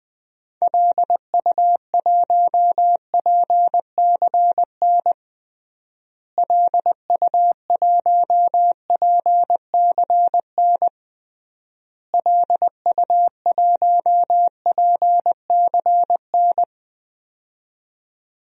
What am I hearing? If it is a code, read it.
Morse code, heard as a keyed tone, about 20 wpm, 700 Hz. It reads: LU1PCN LU1PCN LU1PCN
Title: DXCC-100 Call Signs - 6 Characters - Repeated 3x - 20wpm